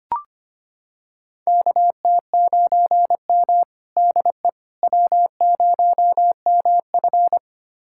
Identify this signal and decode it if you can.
Morse code, heard as a keyed tone, about 25 wpm, 700 Hz. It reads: KT9M DE W0MF
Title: USA Callsign DE Callsign - 4 Characters - 25wpm